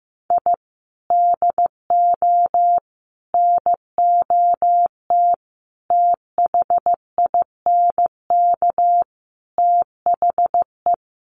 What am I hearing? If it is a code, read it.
Morse code, heard as a keyed tone, about 15 wpm, 700 Hz. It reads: I DO NOT THINK THE
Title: Sets of 5 Words 15wpm